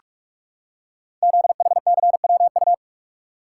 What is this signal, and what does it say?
Morse code, heard as a keyed tone, about 44 wpm, 700 Hz. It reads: ZHCWU
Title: Code Groups - 5 Letters - 45wpm